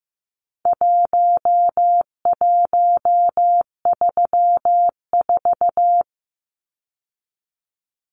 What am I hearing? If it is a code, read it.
Morse code, heard as a keyed tone, about 15 wpm, 700 Hz. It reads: 1134
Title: Numbers - Contest Style - 4-Digit 15wpm